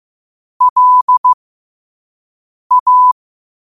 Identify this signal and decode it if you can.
Morse code, heard as a keyed tone, about 15 wpm, spaced as for 5 wpm, 1000 Hz. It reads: LA
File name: qotdfiles / 5wpm